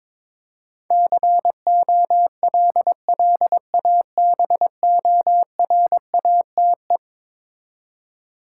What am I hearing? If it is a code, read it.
Morse code, heard as a keyed tone, about 22 wpm, 700 Hz. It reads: COLLABORATE
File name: practice / 22wpm / Eleven Letter Words - 22wpm